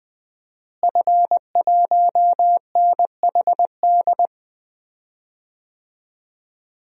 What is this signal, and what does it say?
Morse code, heard as a keyed tone, about 20 wpm, 700 Hz. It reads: F1NHD
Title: DXCC-100 Call Signs - 5 Characters - Repeated 2x - 20wpm